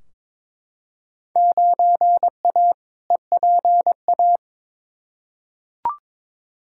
Morse code, heard as a keyed tone, about 22 wpm, 700 Hz. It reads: 9A EPA
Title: Field Day Exchanges Encore - 22wpm